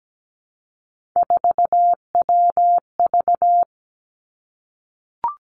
Morse code, heard as a keyed tone, about 17 wpm, 700 Hz. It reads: 4WV
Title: Three Challenging Letter-Number Combinations 17wpm